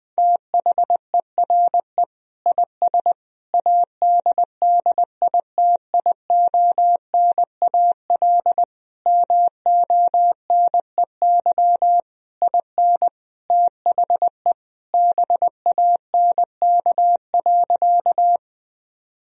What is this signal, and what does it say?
Morse code, heard as a keyed tone, about 20 wpm, 700 Hz. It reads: THERE IS ADDITIONAL MONEY IN THE BANK.
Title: Sentences from Top 1000 Words - Review Part 4 - 20wpm